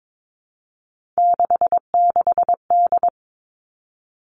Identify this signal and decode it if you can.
Morse code, heard as a keyed tone, about 22 wpm, 700 Hz. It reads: 66D
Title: Three Challenging Letter-Number Combinations 22wpm